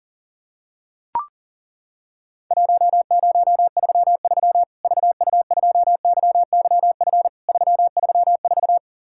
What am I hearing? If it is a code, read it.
Morse code, heard as a keyed tone, about 40 wpm, 700 Hz. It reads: 1033 VU2YYF 334
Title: Worked All Europe DX Contest - QTC Exchange - 40wpm